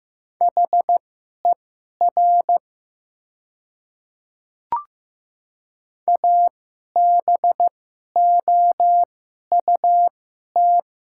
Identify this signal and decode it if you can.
Morse code, heard as a keyed tone, about 15 wpm, spaced as for 11 wpm, 700 Hz. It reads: HER ABOUT
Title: 2x Character Spacing - Top 200 Words - 15wpm